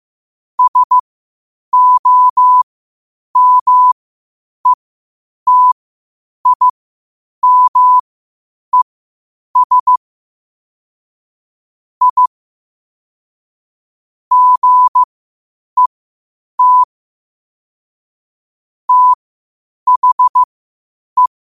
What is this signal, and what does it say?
Morse code, heard as a keyed tone, about 15 wpm, spaced as for 8 wpm, 1000 Hz. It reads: SOMETIMES I GET THE
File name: qotdfiles / 8wpm